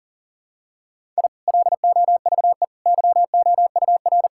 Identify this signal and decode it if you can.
Morse code, heard as a keyed tone, about 40 wpm, 700 Hz. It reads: I LOVE YOUR
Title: Sentences from Top 1000 Words - Review Part 6 - 40wpm